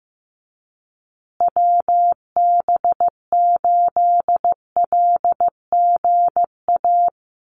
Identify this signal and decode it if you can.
Morse code, heard as a keyed tone, about 15 wpm, 700 Hz. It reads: WB8LGA
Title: USA Call Signs Repeated 2x 15wpm